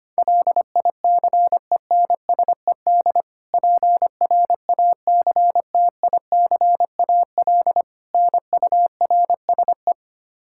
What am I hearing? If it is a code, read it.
Morse code, heard as a keyed tone, about 25 wpm, 700 Hz. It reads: LICENSED PRACTICAL NURSE
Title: Practice QSO - Exchange 3 - More Info - 25wpm